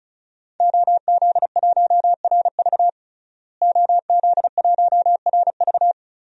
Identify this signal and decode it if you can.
Morse code, heard as a keyed tone, about 36 wpm, 700 Hz. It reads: OZ1RV OZ1RV
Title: DXCC-100 Call Signs - 5 Characters - Repeated 2x - 35wpm